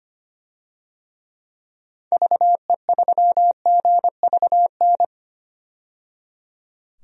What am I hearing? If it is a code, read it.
Morse code, heard as a keyed tone, about 25 wpm, 700 Hz. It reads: VE3GVN
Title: DXCC-100 Call Signs - 6 Characters - 25wpm